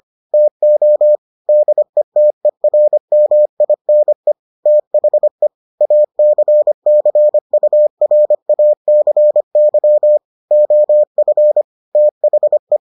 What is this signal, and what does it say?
Morse code, heard as a keyed tone, about 25 wpm, 600 Hz. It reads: TO DETERMINE THE ACCURACY OF THE